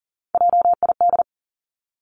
Morse code, heard as a keyed tone, about 40 wpm, 700 Hz. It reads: JID
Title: Three Challenging Letter-Number Combinations 40wpm